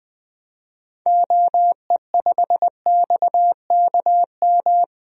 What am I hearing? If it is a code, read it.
Morse code, heard as a keyed tone, about 21 wpm, 700 Hz. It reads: OE5XKM
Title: DXCC-100 Call Signs - 6 Characters - Repeated 3x - 20wpm